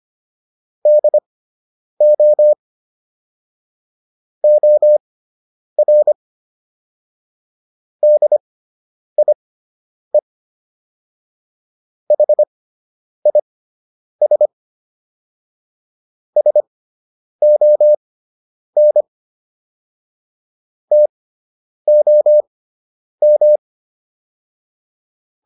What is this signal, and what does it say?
Morse code, heard as a keyed tone, about 25 wpm, spaced as for 9 wpm, 600 Hz. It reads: DO OR DIE HIS SON TOM